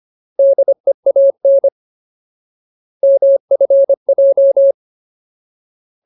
Morse code, heard as a keyed tone, about 25 wpm, 550 Hz. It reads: DEAN MFJ
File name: Ultimate Word List / 25WPM Ultimate